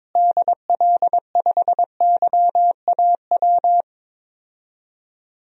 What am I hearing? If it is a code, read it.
Morse code, heard as a keyed tone, about 21 wpm, 700 Hz. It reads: DL5YAW
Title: DXCC-100 Call Signs - 6 Characters - Repeated 3x - 22wpm